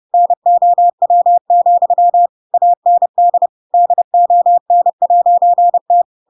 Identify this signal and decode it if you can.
Morse code, heard as a keyed tone, about 30 wpm, 700 Hz. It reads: NOW, AND DON'T
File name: Pinocchio / Pinocchio ch4 30 30 ik1hge